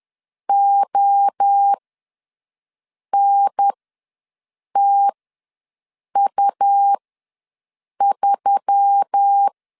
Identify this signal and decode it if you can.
Morse code, heard as a keyed tone, about 10 wpm, spaced as for 6 wpm, 800 Hz. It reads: ONTU3
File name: LEZIONI / 6 PKX0 / 6 P ws 10wpm 3751